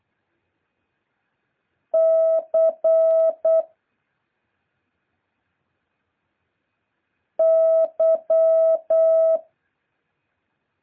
Morse code, heard as a keyed tone, about 8 wpm, 660 Hz. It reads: C Y
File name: jh8cbh / 2023cw